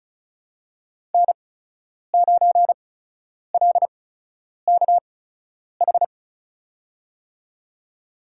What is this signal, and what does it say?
Morse code, heard as a keyed tone, about 36 wpm, spaced as for 10 wpm, 700 Hz. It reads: N9LKH
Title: USA Call Signs - 8x Character Spacing - 35wpm